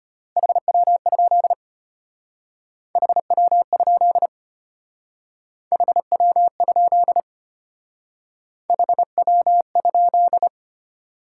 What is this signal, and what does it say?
Morse code, heard as a keyed tone, about 34 wpm, 700 Hz. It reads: HW? HW? HW? HW?